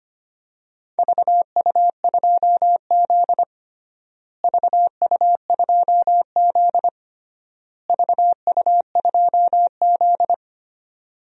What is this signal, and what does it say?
Morse code, heard as a keyed tone, about 25 wpm, 700 Hz. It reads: VU2Z VU2Z VU2Z
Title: DXCC-100 Call Signs - 4 Characters - Repeated 3x - 25wpm